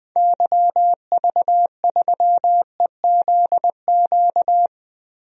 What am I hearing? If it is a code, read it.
Morse code, heard as a keyed tone, about 20 wpm, 700 Hz. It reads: YV3EZQ
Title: DXCC-100 Call Signs - 6 Characters - 20wpm